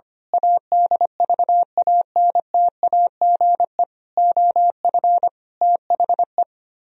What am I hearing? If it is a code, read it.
Morse code, heard as a keyed tone, about 24 wpm, 700 Hz. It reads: ADVANTAGE OF THE
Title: Speed-Racing - Sets of 5 Words 25wpm